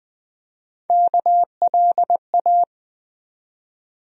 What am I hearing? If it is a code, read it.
Morse code, heard as a keyed tone, about 20 wpm, 700 Hz. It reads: KLA
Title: USA Call Sign Postfixes - 3 Letters 20wpm